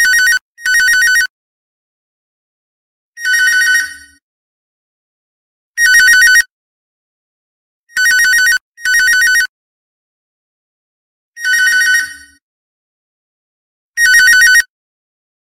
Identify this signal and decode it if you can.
There is no code.